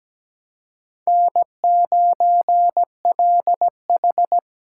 Morse code, heard as a keyed tone, about 20 wpm, 700 Hz. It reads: N9LH
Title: Speed Racing - USA 1x2 Call Signs 17wpm